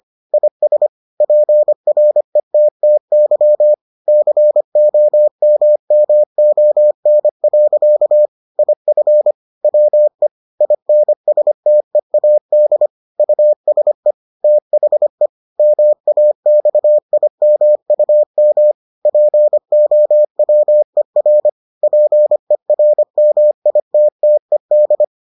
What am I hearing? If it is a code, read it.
Morse code, heard as a keyed tone, about 25 wpm, 600 Hz. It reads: IS PRETTY COMMON. IF WE INSTEAD USE THE MAXIMUM POWER PERMITTED